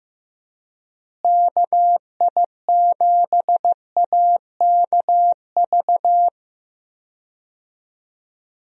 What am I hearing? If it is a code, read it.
Morse code, heard as a keyed tone, about 15 wpm, 700 Hz. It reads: KI7AKV
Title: USA Call Signs - 6 Characters - Repeated 2x - 15wpm